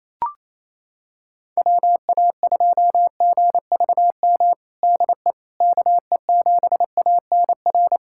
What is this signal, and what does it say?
Morse code, heard as a keyed tone, about 28 wpm, 700 Hz. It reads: WA2GVM DE KE7ANR
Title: USA Callsign DE Callsign - 6 Characters - 28wpm